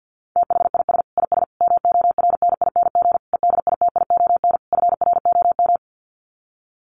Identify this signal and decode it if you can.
Morse code, heard as a keyed tone, about 48 wpm, 700 Hz. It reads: THIS IS MORNING EDITION FROM
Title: Sets of 5 Words Encore2 50wpm